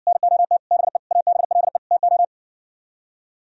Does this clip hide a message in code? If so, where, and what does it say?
Yes: Morse code from 0.1 to 2.2 s, heard as a keyed tone, about 66 wpm, 700 Hz. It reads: NOT BE ABLE TO